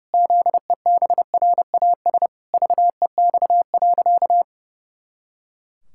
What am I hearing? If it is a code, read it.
Morse code, heard as a keyed tone, about 30 wpm, 700 Hz. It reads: ZEBRAS VEX.
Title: Pangram Challenge v2 - 30wpm